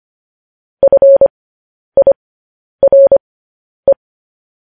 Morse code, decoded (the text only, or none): FIRE